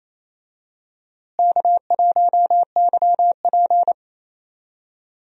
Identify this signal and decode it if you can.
Morse code, heard as a keyed tone, about 28 wpm, 700 Hz. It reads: K1YP
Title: USA Call Signs - 4 Characters - Repeated 3x - 28wpm